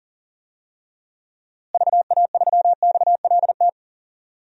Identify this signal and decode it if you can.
Morse code, heard as a keyed tone, about 39 wpm, 700 Hz. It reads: VA3XLT
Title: DXCC-100 Call Signs - 6 Characters - Repeated 3x - 40wpm